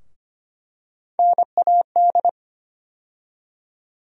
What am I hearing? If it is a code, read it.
Morse code, heard as a keyed tone, about 25 wpm, 700 Hz. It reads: NAD
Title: USA Call Sign Postfixes - 3 Letters 25wpm